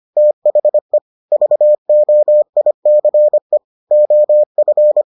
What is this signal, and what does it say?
Morse code, heard as a keyed tone, about 25 wpm, 600 Hz. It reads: THE VOICE OF